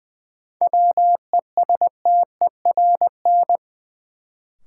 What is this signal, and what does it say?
Morse code, heard as a keyed tone, about 21 wpm, 700 Hz. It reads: WESTERN
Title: Sentences from Top 1000 Words - Review Part 7 - New Word Drill - 20wpm